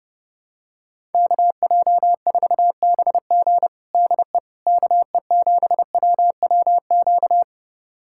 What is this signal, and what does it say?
Morse code, heard as a keyed tone, about 30 wpm, 700 Hz. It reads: KJ4BG DE KE7WWQ
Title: USA Callsign DE Callsign 30wpm